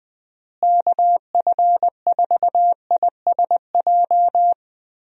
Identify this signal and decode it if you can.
Morse code, heard as a keyed tone, about 20 wpm, 700 Hz. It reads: KF4ISJ
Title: USA Call Signs - 6 Characters - Repeated 3x - 20wpm